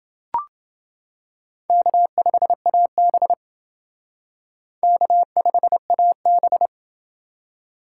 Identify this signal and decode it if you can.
Morse code, heard as a keyed tone, about 29 wpm, 700 Hz. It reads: K5AB K5AB